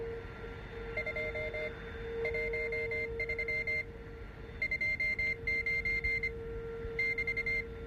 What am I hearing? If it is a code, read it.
Morse code, heard as a keyed tone, about 25 wpm, spaced as for 18 wpm, 2100 Hz, with noise. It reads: 213 29 <BT>